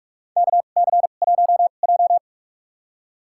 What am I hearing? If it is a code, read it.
Morse code, heard as a keyed tone, about 46 wpm, spaced as for 32 wpm, 700 Hz. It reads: KC1J
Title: USA Call Signs - 4 Characters - 2x Character Spacing - 45wpm